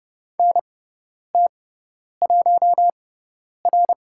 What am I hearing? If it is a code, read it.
Morse code, heard as a keyed tone, about 30 wpm, spaced as for 10 wpm, 700 Hz. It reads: NT1R
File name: Lesson 16g - Letter c - Call Signs 30wpm